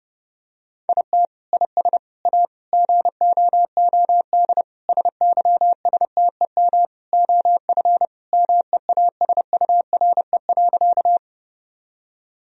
Morse code, heard as a keyed tone, about 30 wpm, 700 Hz. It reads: IT IS A GOOD SYSTEM OF MEASURE.
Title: Sentences from Top 1000 Words - Review Part 3 - 30wpm